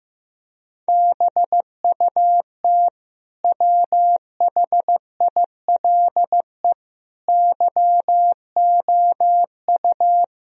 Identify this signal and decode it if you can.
Morse code, heard as a keyed tone, about 15 wpm, 700 Hz. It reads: BUT WHILE YOU
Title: Sentences from Top 1000 Words - Review Part 1 - 15wpm